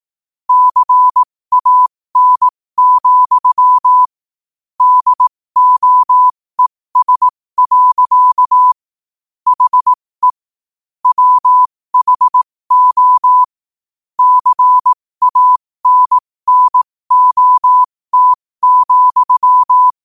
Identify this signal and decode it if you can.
Morse code, heard as a keyed tone, about 18 wpm, spaced as for 15 wpm, 1000 Hz. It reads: CAN, DOES. HE WHO CANNOT,
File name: qotdfiles / 15wpm